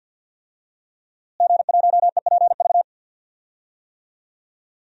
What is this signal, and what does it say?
Morse code, heard as a keyed tone, about 50 wpm, 700 Hz. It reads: G1EPV
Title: DXCC-100 Call Signs - 5 Characters - Repeated 2x - 50wpm